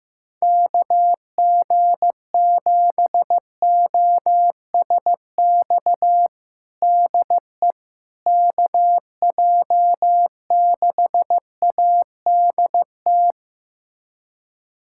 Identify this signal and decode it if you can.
Morse code, heard as a keyed tone, about 15 wpm, 700 Hz. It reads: KG7OSX DE KJ6ADT
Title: USA Callsign DE Callsign 15wpm